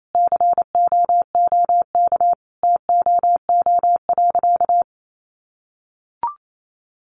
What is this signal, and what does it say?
Morse code, heard as a keyed tone, about 28 wpm, 700 Hz. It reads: COOK TOO.
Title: Sentences from Top 700 Words 28wpm